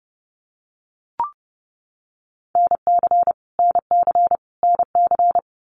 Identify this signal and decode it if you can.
Morse code, heard as a keyed tone, about 30 wpm, 700 Hz. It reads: NC NC NC
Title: US States Repeated 3x 30wpm